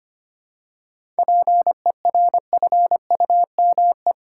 Morse code, heard as a keyed tone, about 25 wpm, 690 Hz. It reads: PERFUME
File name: practice / 25wpm / Easy Words with Hints - 25wpm